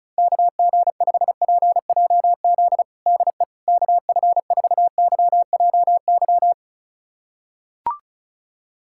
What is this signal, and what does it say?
Morse code, heard as a keyed tone, about 36 wpm, 700 Hz. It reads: KG5PJZ DE KF4YJY
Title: USA Callsign DE Callsign - 6 Characters - 35wpm